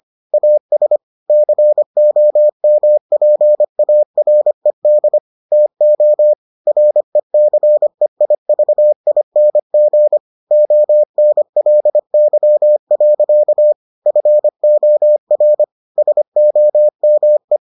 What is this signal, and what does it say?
Morse code, heard as a keyed tone, about 25 wpm, 600 Hz. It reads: AS COMPARED TO RECEIVING ONLY. FOR SOME